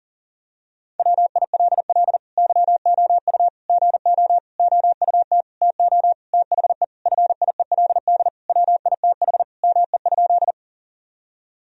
Morse code, heard as a keyed tone, about 37 wpm, 700 Hz. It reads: WILL YOU GO OUT TO THE FIELD WITH ME?
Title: Sentences from Top 1000 Words - Review Part 2 - 40wpm